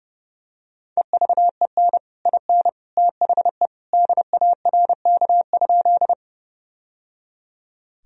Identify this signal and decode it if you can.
Morse code, heard as a keyed tone, about 30 wpm, 700 Hz. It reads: EVEN IN THE DARK?
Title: Sentences from Top 1000 Words - Review Part 1 - 30wpm